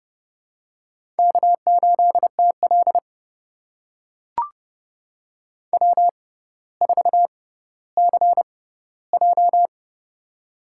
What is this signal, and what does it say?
Morse code, heard as a keyed tone, about 32 wpm, spaced as for 16 wpm, 700 Hz. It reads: K8TL W4CJ
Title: USA Call Signs - 4 Characters - 6x Character Spacing - 30wpm